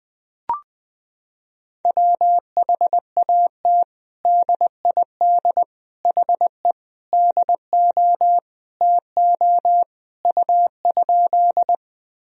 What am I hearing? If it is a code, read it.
Morse code, heard as a keyed tone, about 20 wpm, 700 Hz. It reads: WHAT DID HE DO TO U?
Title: Sentences from Top 100 Words and Abbreviations 20wpm